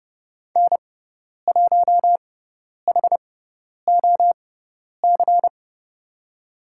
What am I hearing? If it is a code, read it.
Morse code, heard as a keyed tone, about 32 wpm, spaced as for 10 wpm, 700 Hz. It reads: N1HOC
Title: USA Call Signs - 5 Characters - 6x Character Spacing - 30wpm